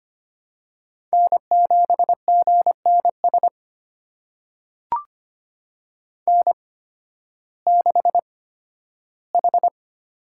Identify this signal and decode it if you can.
Morse code, heard as a keyed tone, about 25 wpm, spaced as for 13 wpm, 700 Hz. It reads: N7GNS N6H
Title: USA Call Signs - 5 Characters - 8x Character Spacing - 25wpm